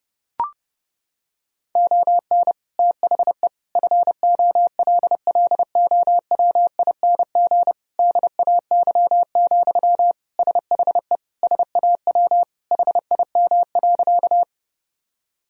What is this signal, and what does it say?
Morse code, heard as a keyed tone, about 31 wpm, 700 Hz. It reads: ON THE FOLLOWING DAY, SHE SAW HIM.
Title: Sentences from Top 1000 Words - Review Part 4 - 30wpm